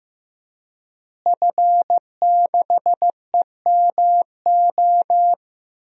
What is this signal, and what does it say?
Morse code, heard as a keyed tone, about 15 wpm, 700 Hz. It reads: F6EMO
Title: DXCC-100 Call Signs - 5 Characters - Repeated 3x - 15wpm